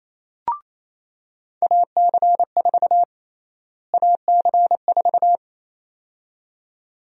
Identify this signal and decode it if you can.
Morse code, heard as a keyed tone, about 28 wpm, 700 Hz. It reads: AC4 AC4
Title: USA Call Sign Prefixes - Repeated 2x - 28wpm